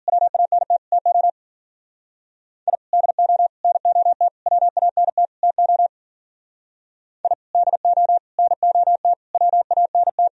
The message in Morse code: WANT TO I DO NOT WANT TO I DO NOT WANT

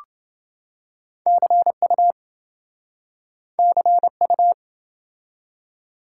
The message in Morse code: CU CU

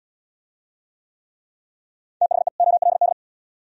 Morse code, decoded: THE OLD

